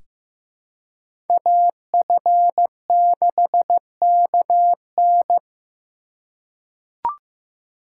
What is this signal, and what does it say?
Morse code, heard as a keyed tone, about 16 wpm, 700 Hz. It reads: AF6KN
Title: USA Call Signs - 3x Character Spacing - 15wpm